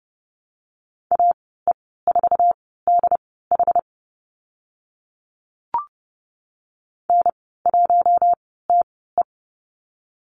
Morse code, heard as a keyed tone, about 30 wpm, spaced as for 17 wpm, 700 Hz. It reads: AE4DH N1TE
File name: Lesson 14g - Number 2 - Call Signs 30wpm